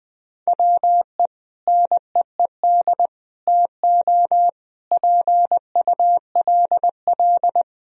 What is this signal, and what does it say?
Morse code, heard as a keyed tone, about 21 wpm, 700 Hz. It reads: WE NEED TO PULL
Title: Sentences from Top 1000 Words - Review Part 2 - 20wpm